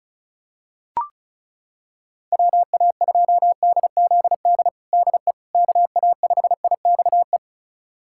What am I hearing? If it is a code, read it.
Morse code, heard as a keyed tone, about 34 wpm, 700 Hz. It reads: WA2DZD DE KA5IXE
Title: USA Callsign DE Callsign - 6 Characters - 35wpm